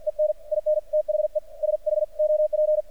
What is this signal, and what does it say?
Morse code, heard as a keyed tone, about 46 wpm, 610 Hz. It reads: EN IN THE SHOP
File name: archived / CRSnet